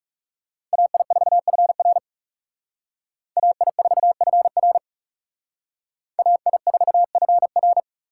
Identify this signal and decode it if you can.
Morse code, heard as a keyed tone, about 36 wpm, 700 Hz. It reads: AI4FR AI4FR AI4FR
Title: Speed Racing - USA Call Signs 30wpm